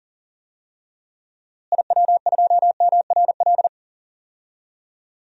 Morse code, heard as a keyed tone, about 39 wpm, 700 Hz. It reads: IW2MRL
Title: DXCC-100 Call Signs - 6 Characters - Repeated 3x - 40wpm